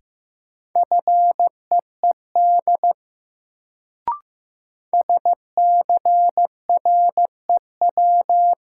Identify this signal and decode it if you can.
Morse code, heard as a keyed tone, about 15 wpm, 700 Hz. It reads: FEED SCREW
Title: Sentences from Top 1600 Words - New Word Drill - 15wpm